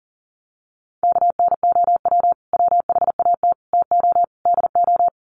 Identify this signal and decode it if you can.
Morse code, heard as a keyed tone, about 40 wpm, 700 Hz. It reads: KNOW WHAT TO DO
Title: Sets of 4 Words 40wpm